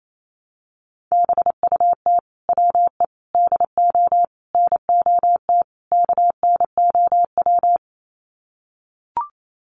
Morse code, heard as a keyed tone, about 28 wpm, 700 Hz. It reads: BUT WE DO NOT KNOW
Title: Sets of 5 Words Encore2 28wpm